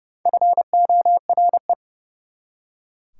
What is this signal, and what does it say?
Morse code, heard as a keyed tone, about 39 wpm, 690 Hz. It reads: FORE